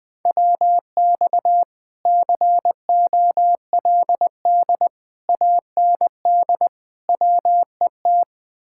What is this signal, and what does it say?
Morse code, heard as a keyed tone, about 20 wpm, 700 Hz. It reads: WX COLD AND WET
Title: Speed-Racing Common QSO Phrases - 17wpm